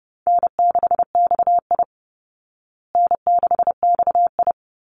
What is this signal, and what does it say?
Morse code, heard as a keyed tone, about 30 wpm, 700 Hz. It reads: N6XI N6XI
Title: Call Signs Repeated 2x - 30wpm